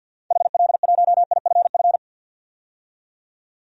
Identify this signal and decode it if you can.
Morse code, heard as a keyed tone, about 47 wpm, 700 Hz. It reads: HL1IFF
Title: DXCC-100 Call Signs - 6 Characters - 50wpm